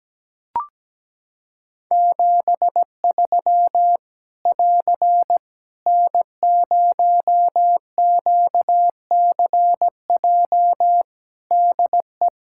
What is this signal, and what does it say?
Morse code, heard as a keyed tone, about 17 wpm, 700 Hz. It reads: 73 <AR> N0QCJ DE
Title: K1USN SST QSO Practice 17wpm